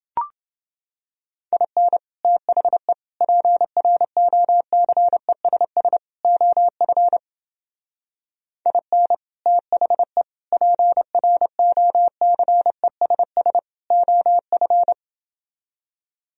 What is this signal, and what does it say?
Morse code, heard as a keyed tone, about 29 wpm, 700 Hz. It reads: IN THE PROCESS OF IN THE PROCESS OF